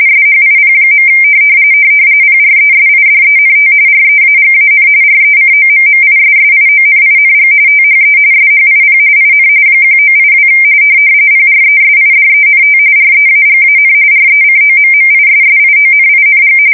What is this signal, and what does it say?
RTTY, about 45.45 baud. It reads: SHIP. PRESIDENT REAGAN RECOMMISSIONING THE SHIP IN 1982 PRESIDENT REAGAN RECOMMISSIONING
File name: fsk170 / battleshipnj